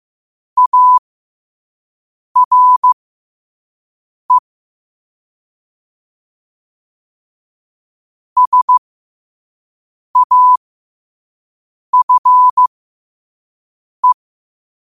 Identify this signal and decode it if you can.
Morse code, heard as a keyed tone, about 15 wpm, spaced as for 5 wpm, 1000 Hz. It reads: ARE SAFE
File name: qotdfiles / 5wpm